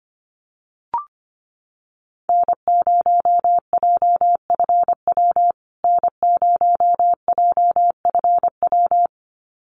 Morse code, heard as a keyed tone, about 25 wpm, 700 Hz. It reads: N0JFW N0JFW
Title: USA Call Signs Repeated 2x 25wpm